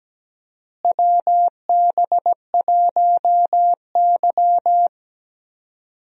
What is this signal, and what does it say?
Morse code, heard as a keyed tone, about 17 wpm, 700 Hz. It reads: WB1Y
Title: USA Call Signs - 4 Characters - Repeated 2x - 17wpm